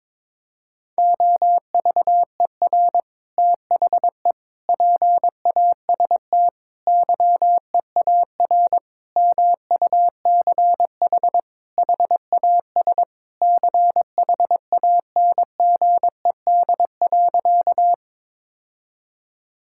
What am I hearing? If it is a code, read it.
Morse code, heard as a keyed tone, about 22 wpm, 700 Hz. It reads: OVER THE PAST YEAR MUCH HAS CHANGED.
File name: practice / 22wpm / Sentences with 3-Word Phrases - 22wpm